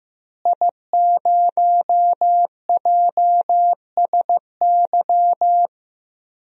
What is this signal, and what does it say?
Morse code, heard as a keyed tone, about 15 wpm, 700 Hz. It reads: I0JSY
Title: DXCC-100 Call Signs - 5 Characters - Repeated 3x - 15wpm